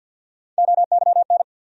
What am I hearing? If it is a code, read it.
Morse code, heard as a keyed tone, about 47 wpm, 700 Hz. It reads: OYN
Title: USA Call Sign Postfixes - 3 Letters 50wpm